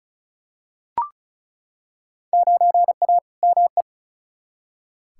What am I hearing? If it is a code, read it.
Morse code, heard as a keyed tone, about 36 wpm, 700 Hz. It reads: 9A ME
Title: Field Day Exchanges Encore - 35wpm